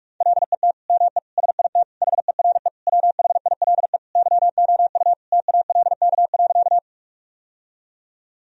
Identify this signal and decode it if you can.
Morse code, heard as a keyed tone, about 42 wpm, 700 Hz. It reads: LET ME SIT HERE WHILE YOU TALK.
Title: Sentences from Top 1000 Words - Review Part 1 - 45wpm